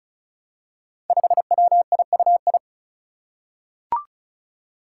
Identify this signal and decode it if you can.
Morse code, heard as a keyed tone, about 34 wpm, 700 Hz. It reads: 5WIUI
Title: Five Challenging Letter-Number Combinations 35wpm